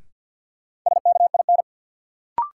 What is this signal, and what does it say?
Morse code, heard as a keyed tone, about 53 wpm, 700 Hz. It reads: SKIN
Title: Sentences from Top 1300 Words - New Word Drill Alt - 50wpm